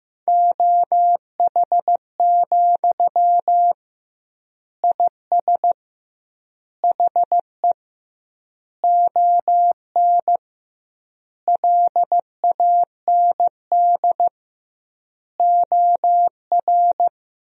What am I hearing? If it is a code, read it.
Morse code, heard as a keyed tone, about 15 wpm, 700 Hz. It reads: OH, IS HE ON LAND OR
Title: 2x Word-Spacing - Sentences from Top 500 Words - 15wpm